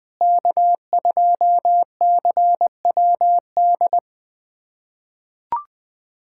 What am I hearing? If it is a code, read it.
Morse code, heard as a keyed tone, about 20 wpm, 700 Hz. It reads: K2CWD
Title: USA Callsign DE Callsign - 5 Characters - 20wpm